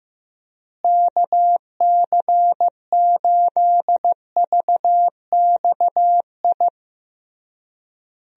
Morse code, heard as a keyed tone, about 15 wpm, 700 Hz. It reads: KC8VXI
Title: USA Call Signs - 6 Characters - Repeated 3x - 15wpm